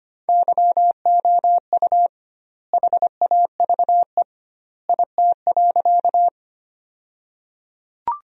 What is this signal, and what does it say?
Morse code, heard as a keyed tone, about 25 wpm, 700 Hz. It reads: YOU HAVE IT.
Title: 2x Word-Spacing - Sentences from Top 100 Words - 25wpm